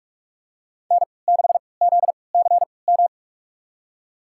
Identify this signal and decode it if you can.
Morse code, heard as a keyed tone, about 46 wpm, spaced as for 25 wpm, 700 Hz. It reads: N6ZCM
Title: USA Call Signs - 5 Characters - 3x Character Spacing - 45wpm